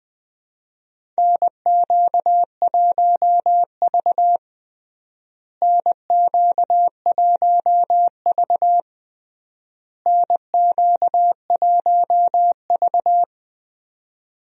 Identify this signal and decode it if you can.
Morse code, heard as a keyed tone, about 20 wpm, 700 Hz. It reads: NQ1V NQ1V NQ1V
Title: USA Call Signs - 4 Characters - Repeated 3x - 20wpm